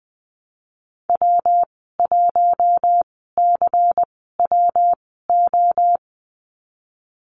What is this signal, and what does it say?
Morse code, heard as a keyed tone, about 20 wpm, spaced as for 14 wpm, 700 Hz. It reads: W1CWO
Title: 161 Lesson 26g - Letter g - Call Signs 20wpm